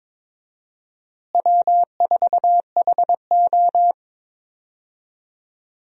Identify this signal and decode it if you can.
Morse code, heard as a keyed tone, about 22 wpm, 700 Hz. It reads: W4HO
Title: USA Call Signs - 4 Characters - Repeated 2x - 22wpm